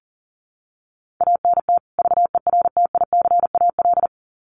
Morse code, heard as a keyed tone, about 40 wpm, 700 Hz. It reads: ANT VERTICAL
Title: Common QSO Phrases 40wpm